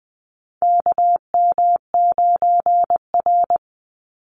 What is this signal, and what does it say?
Morse code, heard as a keyed tone, about 20 wpm, 700 Hz. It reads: KM9R
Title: Call Signs - 20wpm